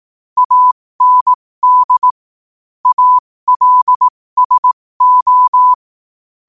Morse code, heard as a keyed tone, about 18 wpm, spaced as for 15 wpm, 1000 Hz. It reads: AND ALSO
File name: qotdfiles / 15wpm